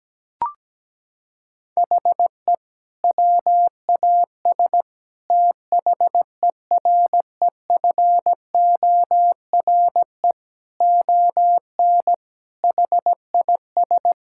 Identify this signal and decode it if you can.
Morse code, heard as a keyed tone, about 17 wpm, 700 Hz. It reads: HE WAS THEREFORE ON HIS
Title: Sentences from Top 1000 Words - Review Part 5 - 17wpm